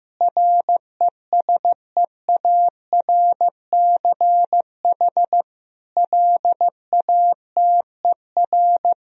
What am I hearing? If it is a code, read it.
Morse code, heard as a keyed tone, about 15 wpm, 700 Hz. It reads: RESEARCH LATER
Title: Sentences from Top 1000 Words - Review Part 4 - 15wpm